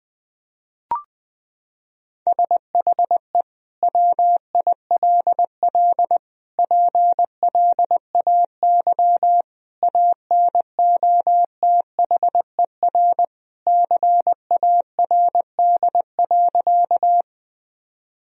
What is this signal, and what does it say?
Morse code, heard as a keyed tone, about 21 wpm, 700 Hz. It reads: SHE WILL PLAY ANOTHER CARD.
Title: Sentences from Top 1000 Words - Review Part 4 - 20wpm